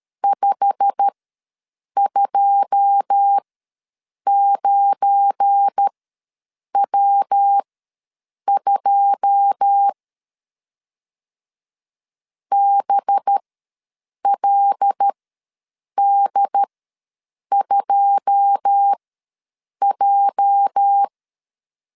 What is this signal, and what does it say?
Morse code, heard as a keyed tone, about 13 wpm, spaced as for 7 wpm, 800 Hz. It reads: 529W2 BLD2J
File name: LEZIONI / 9 2 ws 12wpm 3406